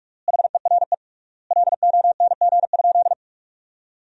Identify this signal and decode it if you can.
Morse code, heard as a keyed tone, about 46 wpm, 700 Hz. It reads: HERE LONG?
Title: 3x Word-Spacing - Sentences from Top 200 Words - 45wpm